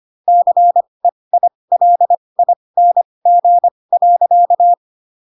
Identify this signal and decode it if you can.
Morse code, heard as a keyed tone, about 25 wpm, spaced as for 20 wpm, 700 Hz. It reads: CEILING.
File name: Pinocchio / Pinocchio ch1 15 25 ik1hge